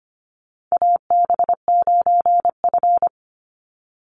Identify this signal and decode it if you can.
Morse code, heard as a keyed tone, about 25 wpm, 690 Hz. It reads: AB9F
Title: USA Call Signs 25wpm